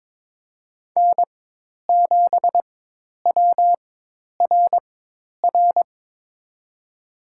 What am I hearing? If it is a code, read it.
Morse code, heard as a keyed tone, about 22 wpm, spaced as for 10 wpm, 700 Hz. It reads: N7WRR
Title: USA Call Signs - 5 Characters - 4x Character Spacing - 22wpm